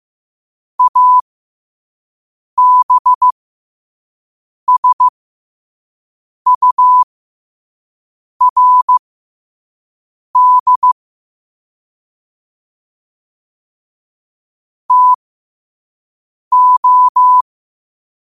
Morse code, heard as a keyed tone, about 15 wpm, spaced as for 5 wpm, 1000 Hz. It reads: ABSURD TO
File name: qotdfiles / 5wpm